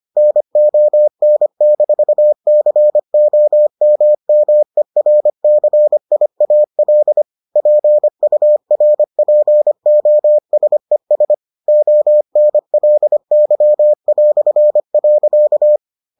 Morse code, heard as a keyed tone, about 25 wpm, 600 Hz. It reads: NON-COMMERCIAL PURPOSES ONLY".